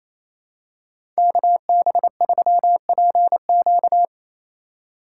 Morse code, heard as a keyed tone, about 28 wpm, 700 Hz. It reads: KB3PQ
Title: USA Call Signs - 5 Characters - 4x Character Spacing - 28wpm